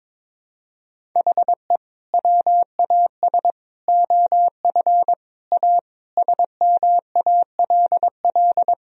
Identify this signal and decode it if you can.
Morse code, heard as a keyed tone, about 22 wpm, 700 Hz. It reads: HE WAS OF A SMALL
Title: Sentences from Top 1000 Words - Review Part 1 - 22wpm